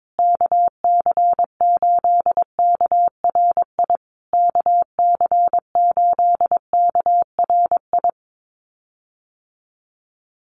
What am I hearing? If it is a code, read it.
Morse code, heard as a keyed tone, about 22 wpm, 700 Hz. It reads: KC8KRI KC8KRI
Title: USA Call Signs Repeated 3x 22wpm